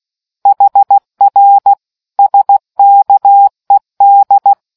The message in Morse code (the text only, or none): HR SKED